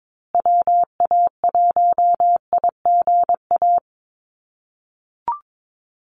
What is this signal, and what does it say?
Morse code, heard as a keyed tone, about 22 wpm, 700 Hz. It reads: WA1IGA
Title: USA Call Signs Repeated 3x 22wpm